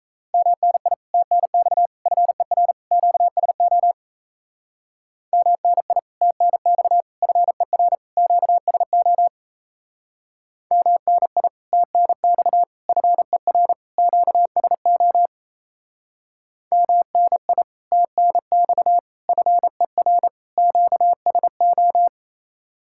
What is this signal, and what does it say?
Morse code, heard as a keyed tone, about 35 wpm, 700 Hz. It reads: MNI TNX FER QSO MNI TNX FER QSO MNI TNX FER QSO MNI TNX FER QSO
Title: Speed-Racing Common QSO Phrases Encore - 28wpm